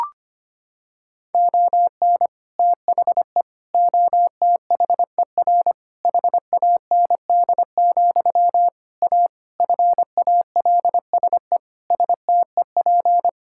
Morse code, heard as a keyed tone, about 25 wpm, 700 Hz. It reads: ON THE OTHER HAND, A FALSE STEP
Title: Sentences from Top 1600 Words 25wpm